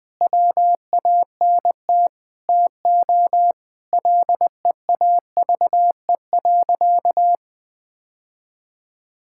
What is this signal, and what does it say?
Morse code, heard as a keyed tone, about 20 wpm, 700 Hz. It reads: WANT TO LEAVE.
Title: Sentences with 3-Word Phrases - 20wpm